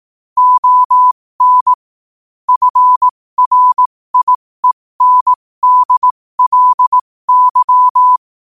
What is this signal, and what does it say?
Morse code, heard as a keyed tone, about 18 wpm, spaced as for 15 wpm, 1000 Hz. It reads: ON FRIENDLY
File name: qotdfiles / 15wpm